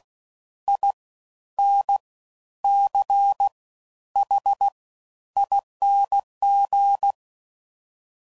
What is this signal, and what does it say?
Morse code, heard as a keyed tone, about 16 wpm, 800 Hz. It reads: I N C H ING
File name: Suffixes / Special INGs / Fast Ings